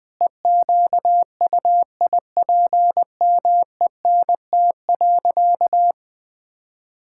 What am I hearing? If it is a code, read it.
Morse code, heard as a keyed tone, about 20 wpm, 700 Hz. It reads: EQUIPMENT.
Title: Sentences from Top 1000 Words - Review Part 5 - 20wpm